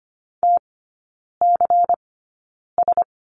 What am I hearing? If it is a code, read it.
Morse code, heard as a keyed tone, about 25 wpm, spaced as for 9 wpm, 700 Hz. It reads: TCS
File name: course / 25:10wpm / 093 Lesson 16e - Letter c - Three Characters 25wpm